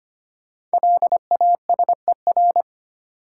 Morse code, heard as a keyed tone, about 25 wpm, 700 Hz. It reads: LASER
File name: 5 Letter Words - First 3 Letters Given - 25wpm